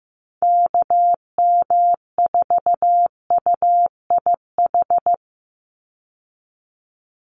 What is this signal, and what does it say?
Morse code, heard as a keyed tone, about 15 wpm, 700 Hz. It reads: KM4UIH
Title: USA Call Signs 15wpm